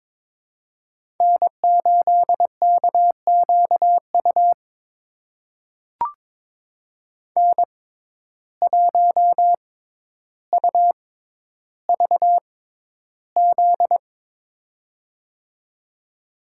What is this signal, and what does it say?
Morse code, heard as a keyed tone, about 22 wpm, spaced as for 11 wpm, 700 Hz. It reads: N8KQU N1UVZ
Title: USA Call Signs - 5 Characters - 6x Character Spacing - 22wpm